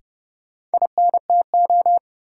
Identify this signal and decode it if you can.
Morse code, heard as a keyed tone, about 30 wpm, 700 Hz. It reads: INTO
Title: Sentences from Top 600 Words - New Word Drill - 30wpm